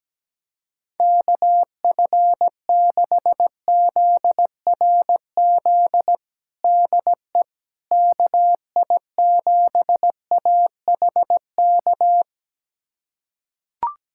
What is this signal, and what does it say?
Morse code, heard as a keyed tone, about 17 wpm, 700 Hz. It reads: KF6ZRZ DE KI7AHK
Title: USA Callsign DE Callsign 17wpm